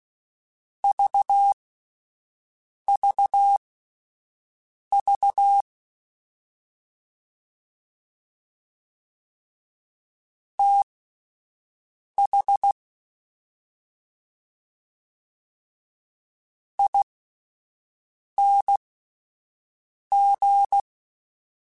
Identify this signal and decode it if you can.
Morse code, heard as a keyed tone, about 16 wpm, spaced as for 5 wpm, 800 Hz. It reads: VVV TH ING
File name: Misc / Misc-mp3s / Jumps @5